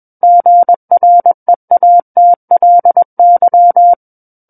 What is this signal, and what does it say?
Morse code, heard as a keyed tone, about 21 wpm, 700 Hz. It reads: GREATLY
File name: Book One / 12-Chapter 12 - 21 WPM